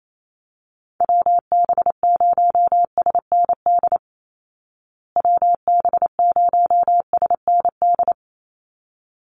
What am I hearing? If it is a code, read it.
Morse code, heard as a keyed tone, about 28 wpm, 700 Hz. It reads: WB0SND WB0SND
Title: Call Signs Repeated 3x - 28wpm